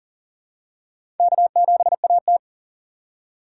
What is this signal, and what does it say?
Morse code, heard as a keyed tone, about 40 wpm, 700 Hz. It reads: K7AT
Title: USA Call Signs - 4 Characters - 4x Character Spacing - 40wpm